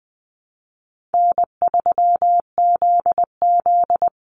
Morse code, heard as a keyed tone, about 20 wpm, 700 Hz. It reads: N3ZZ
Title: Easy Call Signs Repeated 3x - 20wpm